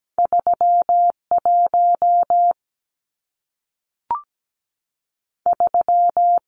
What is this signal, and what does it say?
Morse code, heard as a keyed tone, about 17 wpm, 700 Hz. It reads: 3 1 3
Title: Numbers - 2-Digit 17wpm